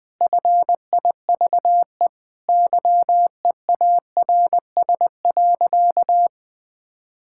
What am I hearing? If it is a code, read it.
Morse code, heard as a keyed tone, about 20 wpm, 700 Hz. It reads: FIVE YEARS.
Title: Sentences with 5-Word Phrases - 20wpm